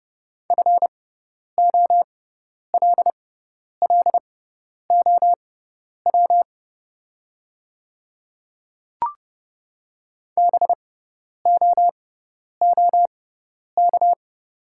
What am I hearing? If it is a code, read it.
Morse code, heard as a keyed tone, about 32 wpm, spaced as for 10 wpm, 700 Hz. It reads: FOLLOW BOOK
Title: 6x Character Spacing - Top 300 Words - 30wpm